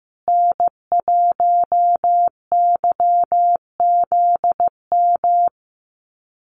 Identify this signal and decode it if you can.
Morse code, heard as a keyed tone, about 15 wpm, 700 Hz. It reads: N1YZM
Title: USA Call Signs Repeated 3x 15wpm